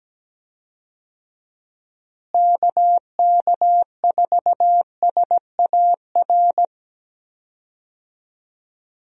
Morse code, heard as a keyed tone, about 17 wpm, 700 Hz. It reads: KK4SAR
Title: USA Call Signs - 6 Characters - Repeated 3x - 17wpm